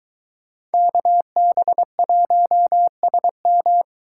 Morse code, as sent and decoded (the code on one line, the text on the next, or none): -.- -... .---- ... --
KB1SM